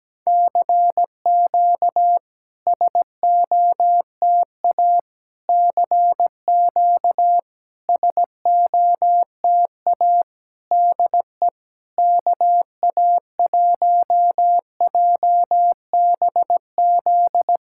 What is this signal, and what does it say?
Morse code, heard as a keyed tone, about 17 wpm, 700 Hz. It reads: CQ SOTA CQ SOTA DE KA1JBZ
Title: SOTA QSO Practice 17wpm